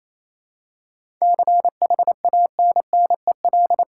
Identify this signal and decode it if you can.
Morse code, heard as a keyed tone, about 28 wpm, 700 Hz. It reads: CHANNEL
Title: Seven Letter Words - 28wpm